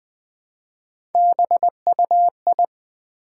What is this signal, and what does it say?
Morse code, heard as a keyed tone, about 20 wpm, 700 Hz. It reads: BUI
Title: Build A Word - 1 Letter At A Time - 3 to 5-Letter Words - 20wpm